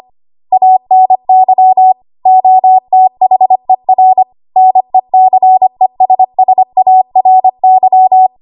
Morse code, heard as a keyed tone, about 25 wpm, 750 Hz. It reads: ANY OTHER NECESSARY
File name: Morse / 25WPM